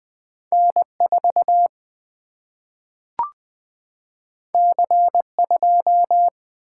Morse code, heard as a keyed tone, about 20 wpm, 700 Hz. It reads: N 4 C 2
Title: Single-Letter Single-Number 20wpm